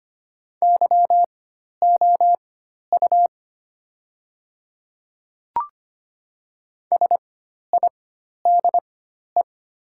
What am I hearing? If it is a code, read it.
Morse code, heard as a keyed tone, about 25 wpm, spaced as for 12 wpm, 700 Hz. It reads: YOU SIDE
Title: 4x Character Spacing - Top 100 Words - 25wpm